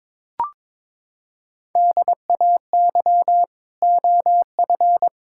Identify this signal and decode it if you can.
Morse code, heard as a keyed tone, about 22 wpm, 700 Hz. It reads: DAY OF
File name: practice / 22wpm / Sets of 8 Words - 22wpm